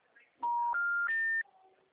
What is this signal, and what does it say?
Special information tone.
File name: out- -2010- - - .439